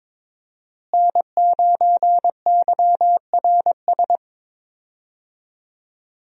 Morse code, heard as a keyed tone, about 22 wpm, 700 Hz. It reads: N9YRS
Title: USA Call Signs - 5 Characters - Repeated 3x - 22wpm